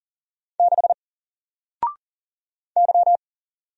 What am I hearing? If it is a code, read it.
Morse code, heard as a keyed tone, about 39 wpm, 700 Hz. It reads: 6 Y
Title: Single Character (Letter:Number:Punctuation:Prosign) 40wpm